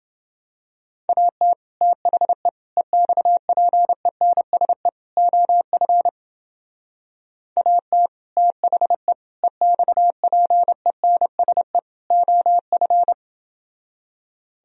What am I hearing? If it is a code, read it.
Morse code, heard as a keyed tone, about 30 wpm, 700 Hz. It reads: AT THE EXPENSE OF AT THE EXPENSE OF